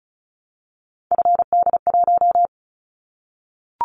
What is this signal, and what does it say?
Morse code, heard as a keyed tone, about 36 wpm, 700 Hz. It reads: FD1
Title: Callsign Prefixes for Top 50 DXCC - 35wpm